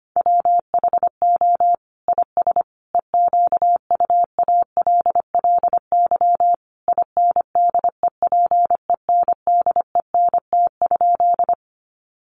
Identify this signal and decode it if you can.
Morse code, heard as a keyed tone, about 25 wpm, 700 Hz. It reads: WHO IS EQUALLY INDEPENDENT?
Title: Sentences from Top 1400 Words 25wpm